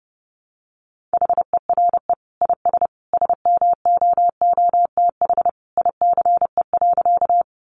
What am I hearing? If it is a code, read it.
Morse code, heard as a keyed tone, about 29 wpm, 700 Hz. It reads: HERE IS SMOOTH ICE.
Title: Sentences from Top 1400 Words 30wpm